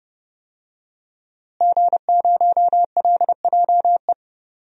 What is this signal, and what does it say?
Morse code, heard as a keyed tone, about 30 wpm, 700 Hz. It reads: G0LJE
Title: DXCC-100 Call Signs - 5 Characters - Repeated 2x - 30wpm